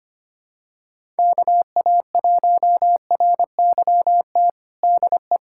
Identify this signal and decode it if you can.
Morse code, heard as a keyed tone, about 25 wpm, 700 Hz. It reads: KA1RYT DE
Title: USA Callsign DE Callsign - 6 Characters - 25wpm